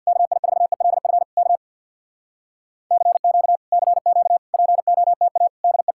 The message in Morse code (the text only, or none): KI4ELF K CQ CQ POTA DE